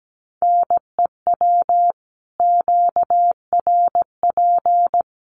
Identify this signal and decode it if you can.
Morse code, heard as a keyed tone, about 17 wpm, 700 Hz. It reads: NEW QRP
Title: Common QSO Phrases 17wpm